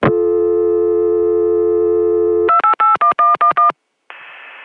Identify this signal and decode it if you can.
DTMF, heard as telephone touch tones, tones of about 120 ms. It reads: 2881111